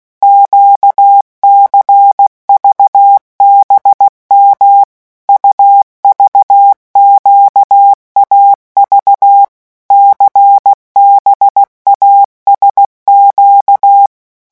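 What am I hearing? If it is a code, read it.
Morse code, heard as a keyed tone, about 16 wpm, 800 Hz. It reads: QCVBM UVQAV CBASQ
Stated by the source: Morse letter-group revision drill at 15 words per minute.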